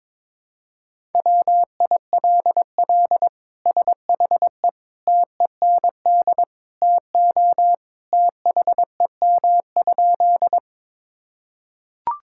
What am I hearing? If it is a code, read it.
Morse code, heard as a keyed tone, about 22 wpm, 700 Hz. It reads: WILL SHE TEND TO THEM?
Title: Sentences from Top 1000 Words - Review Part 5 - 22wpm